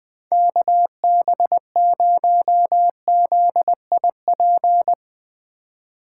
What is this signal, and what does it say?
Morse code, heard as a keyed tone, about 20 wpm, 700 Hz. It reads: KB0ZIP